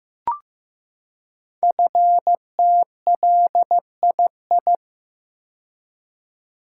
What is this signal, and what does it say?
Morse code, heard as a keyed tone, about 15 wpm, 700 Hz. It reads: FTLII
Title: Code Groups - 5 Letters - 15wpm